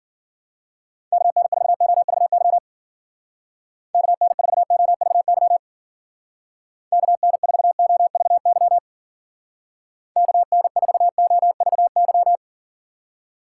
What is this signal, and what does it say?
Morse code, heard as a keyed tone, about 51 wpm, 700 Hz. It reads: KN4OVY KN4OVY KN4OVY KN4OVY